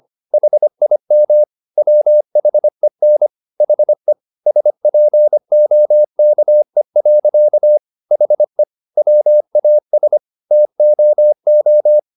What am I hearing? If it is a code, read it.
Morse code, heard as a keyed tone, about 25 wpm, 600 Hz. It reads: HIM WHEN HE SPOKE. HE WAS TOO
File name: the old man and the sea part 1 25wpm